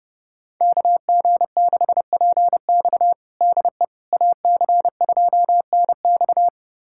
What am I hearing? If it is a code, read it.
Morse code, heard as a keyed tone, about 30 wpm, 700 Hz. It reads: KG6PX DE AC2NX
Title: USA Callsign DE Callsign - 5 Characters - 30wpm